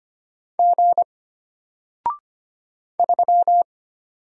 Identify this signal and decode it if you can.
Morse code, heard as a keyed tone, about 25 wpm, 700 Hz. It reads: G 3
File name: Single-Letter-Number-25wpm